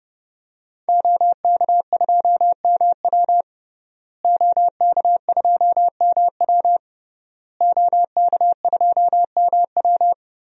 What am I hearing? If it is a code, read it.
Morse code, heard as a keyed tone, about 31 wpm, 700 Hz. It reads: OK2MW OK2MW OK2MW
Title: DXCC-100 Call Signs - 5 Characters - Repeated 3x - 30wpm